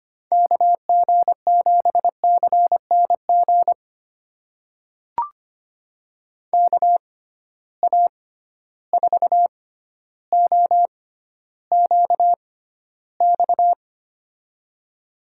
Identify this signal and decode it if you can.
Morse code, heard as a keyed tone, about 25 wpm, spaced as for 13 wpm, 700 Hz. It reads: KG7CNG KA4OQX
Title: USA Call Signs - 6 Characters - 6x Character Spacing - 25wpm